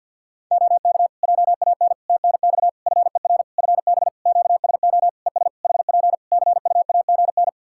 Morse code, heard as a keyed tone, about 52 wpm, 700 Hz. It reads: OK JAN TNX FER FB QSO ES HP CUAGN
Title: K1USN SST QSO Practice 50wpm